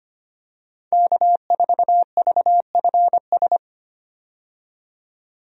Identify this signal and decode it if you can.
Morse code, heard as a keyed tone, about 25 wpm, 700 Hz. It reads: K4VFS
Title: USA Call Signs - 5 Characters - 25wpm